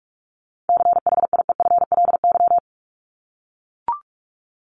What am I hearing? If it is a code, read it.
Morse code, heard as a keyed tone, about 44 wpm, 700 Hz. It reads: CHIEFLY
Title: LY Suffix - Top 100 Words with Base Words 45wpm